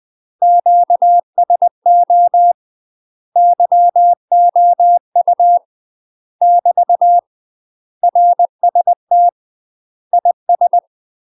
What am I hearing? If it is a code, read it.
Morse code, heard as a keyed tone, about 20 wpm, 700 Hz. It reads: QSO YOU <BT> RST IS
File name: UK MASTER QSOs TRAINING / 20WPM / UK MASTER QSOs 0003